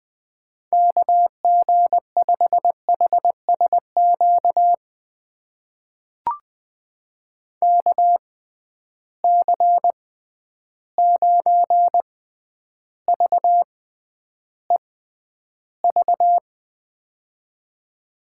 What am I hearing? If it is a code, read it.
Morse code, heard as a keyed tone, about 20 wpm, spaced as for 10 wpm, 700 Hz. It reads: KG5HSQ KC9VEV
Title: USA Call Signs - 6 Characters - 6x Character Spacing - 20wpm